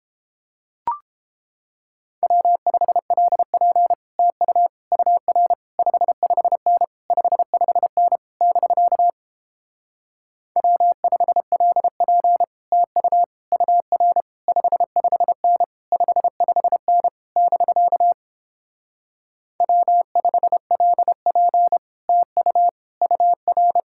Morse code, heard as a keyed tone, about 29 wpm, 700 Hz. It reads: W5LP TU UR 55N 55N <BK> W5LP TU UR 55N 55N <BK> W5LP TU UR